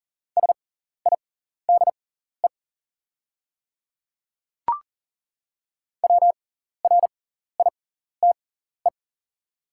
Morse code, heard as a keyed tone, about 40 wpm, spaced as for 14 wpm, 700 Hz. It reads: SIDE WRITE
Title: 6x Character Spacing - Top 200 Words - 40wpm